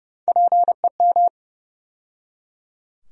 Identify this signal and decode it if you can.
Morse code, heard as a keyed tone, about 32 wpm, 700 Hz. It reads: PEM
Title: USA Call Sign Postfixes - 3 Letters 30wpm